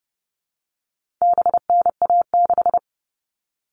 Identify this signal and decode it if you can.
Morse code, heard as a keyed tone, about 30 wpm, 700 Hz. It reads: BNA6
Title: All Cut Numbers - 4-Digit 30wpm